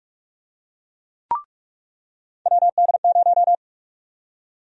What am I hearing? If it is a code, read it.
Morse code, heard as a keyed tone, about 46 wpm, 700 Hz. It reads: WD0
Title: USA Call Sign Prefixes - 45wpm